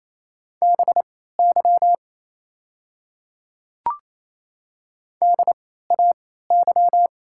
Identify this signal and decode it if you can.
Morse code, heard as a keyed tone, about 27 wpm, spaced as for 16 wpm, 690 Hz. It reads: BY DAY
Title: 3x Character Spacing - Top 100 Words - 28wpm